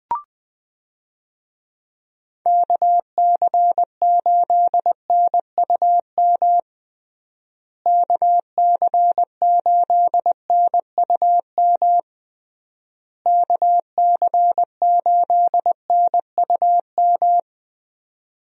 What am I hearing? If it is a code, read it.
Morse code, heard as a keyed tone, about 20 wpm, 700 Hz. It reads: KC8NUM KC8NUM KC8NUM
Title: USA Call Signs - 6 Characters - Repeated 3x - 20wpm